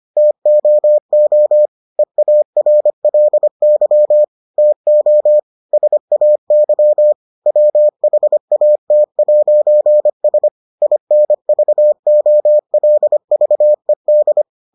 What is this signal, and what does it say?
Morse code, heard as a keyed tone, about 25 wpm, 600 Hz. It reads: TOO EARLY TO SAY WHAT'S INVOLVED